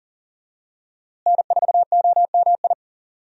Morse code, heard as a keyed tone, about 40 wpm, 700 Hz. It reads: N4OMI
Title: USA Call Signs - 5 Characters - Repeated 2x - 40wpm